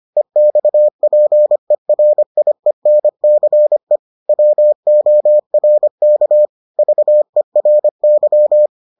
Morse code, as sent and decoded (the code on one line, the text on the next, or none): . -..- .--. .-. .. -. -.-. / .-- --- .-. -.- / ...- . .-. -.--
EXPERIENCE WORK VERY